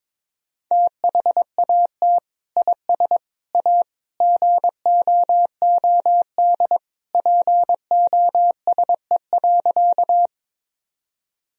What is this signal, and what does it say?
Morse code, heard as a keyed tone, about 22 wpm, 700 Hz. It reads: THAT IS A GOOD POSE.
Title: Sentences from Top 1000 Words - Review Part 2 - 22wpm